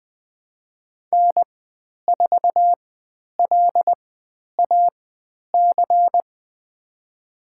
Morse code, heard as a keyed tone, about 20 wpm, spaced as for 10 wpm, 700 Hz. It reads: N4LAC
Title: 095 Lesson 16g - Letter c - Call Signs 20wpm